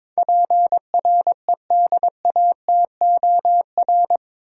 Morse code, heard as a keyed tone, about 22 wpm, 700 Hz. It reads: PREDATOR
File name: practice / 22wpm / Eight Letter Words - 22wpm